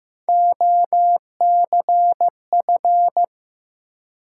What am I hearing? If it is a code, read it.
Morse code, heard as a keyed tone, about 15 wpm, 700 Hz. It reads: OCF
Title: Top 180 QSO Elements 15wpm